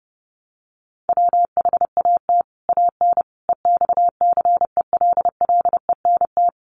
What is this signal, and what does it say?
Morse code, heard as a keyed tone, about 30 wpm, 700 Hz. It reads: WHAT AN EXCELLENT
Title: Sentences from Top 1000 Words 30wpm